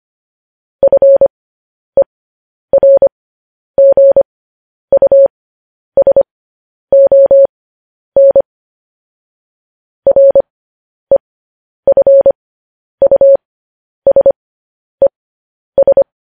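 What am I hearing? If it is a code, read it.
Morse code, heard as a keyed tone, about 25 wpm, spaced as for 10 wpm, 550 Hz. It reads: FERGUSON REFUSES
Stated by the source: Morse code transcription of news headlines.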